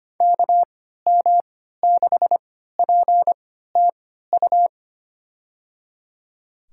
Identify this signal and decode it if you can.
Morse code, heard as a keyed tone, about 25 wpm, spaced as for 14 wpm, 700 Hz. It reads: KM6PTU
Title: USA Call Signs - 6 Characters - 3x Character Spacing - 25wpm